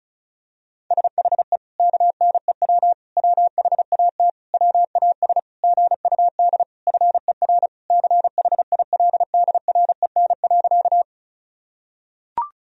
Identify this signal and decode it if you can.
Morse code, heard as a keyed tone, about 36 wpm, 700 Hz. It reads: SHE KNEW WHAT WAS GUD FER CHILDREN.
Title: Sentences from Top 400 Words and Abbreviations 35wpm